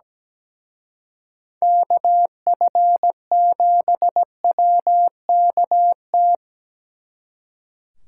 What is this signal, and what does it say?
Morse code, heard as a keyed tone, about 20 wpm, 700 Hz. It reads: KF7WKT
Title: Speed Racing - USA Call Signs 17wpm